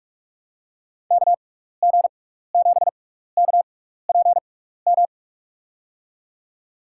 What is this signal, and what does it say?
Morse code, heard as a keyed tone, about 47 wpm, spaced as for 16 wpm, 700 Hz. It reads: KG7KPM
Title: USA Call Signs - 6 Characters - 6x Character Spacing - 45wpm